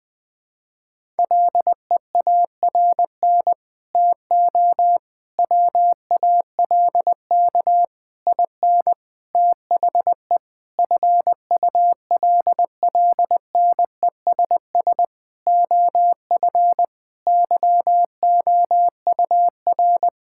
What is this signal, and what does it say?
Morse code, heard as a keyed tone, about 19 wpm, 700 Hz. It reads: LEARN TO WALK IN THE FULLNESS OF YOUR
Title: Sets of 9 Words - 20wpm